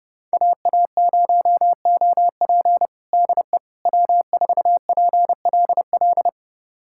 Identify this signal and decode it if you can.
Morse code, heard as a keyed tone, about 30 wpm, 700 Hz. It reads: AA0OP DE W4PLL
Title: USA Callsign DE Callsign - 5 Characters - 30wpm